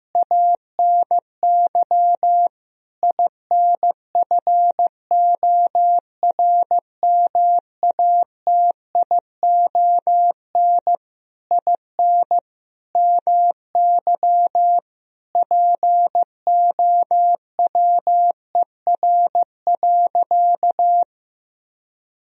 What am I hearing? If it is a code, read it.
Morse code, heard as a keyed tone, about 15 wpm, 700 Hz. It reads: ANY INFORMATION IN MY POWER.
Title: Sentences from Top 1000 Words - Review Part 3 - 15wpm